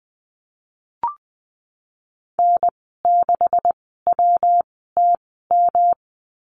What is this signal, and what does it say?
Morse code, heard as a keyed tone, about 20 wpm, spaced as for 15 wpm, 700 Hz. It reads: N6WTM
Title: 161 Lesson 26g - Letter g - Call Signs 20wpm